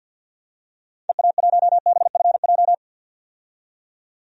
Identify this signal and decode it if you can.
Morse code, heard as a keyed tone, about 49 wpm, 700 Hz. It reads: EA1BFJ
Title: DXCC-100 Call Signs - 6 Characters - 50wpm